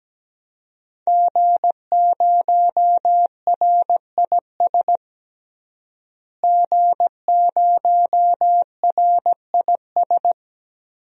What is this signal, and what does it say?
Morse code, heard as a keyed tone, about 17 wpm, 700 Hz. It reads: G0RIS G0RIS
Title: DXCC-100 Call Signs - 5 Characters - Repeated 3x - 17wpm